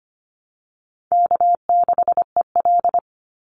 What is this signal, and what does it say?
Morse code, heard as a keyed tone, about 25 wpm, 700 Hz. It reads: K6EL
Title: NA SOTA Chaser Callsigns 25wpm